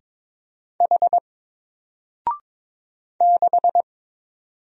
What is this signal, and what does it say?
Morse code, heard as a keyed tone, about 22 wpm, 700 Hz. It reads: H 6
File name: Single-Letter-Number-22wpm